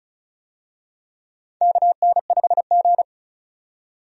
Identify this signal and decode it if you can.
Morse code, heard as a keyed tone, about 34 wpm, 700 Hz. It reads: KN5G
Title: USA Call Signs - 4 Characters - 35wpm